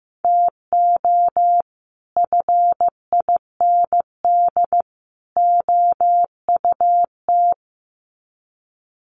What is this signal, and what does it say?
Morse code, heard as a keyed tone, about 15 wpm, 700 Hz. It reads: TO FIND OUT
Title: Sets of 3 Words 15wpm